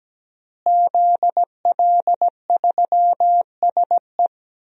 Morse code, heard as a keyed tone, about 17 wpm, 700 Hz. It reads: ZL3SE
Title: DXCC-100 Call Signs - 5 Characters - Repeated 2x - 17wpm